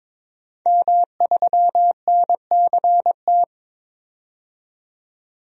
Morse code, heard as a keyed tone, about 23 wpm, 700 Hz. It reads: M3NCT
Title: DXCC-100 Call Signs - 5 Characters - 22wpm